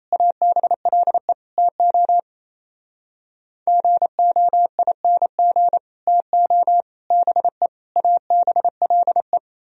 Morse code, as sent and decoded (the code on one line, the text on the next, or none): .- -... .-.. / - --- / --. --- .. -. --. / - --- / -... / .- -... .-.. .
ABLE TO GOING TO BE ABLE